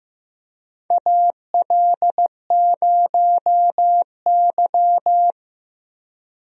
Morse code, heard as a keyed tone, about 16 wpm, 700 Hz. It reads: AL0Y
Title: USA Call Signs - 4 Characters - Repeated 3x - 15wpm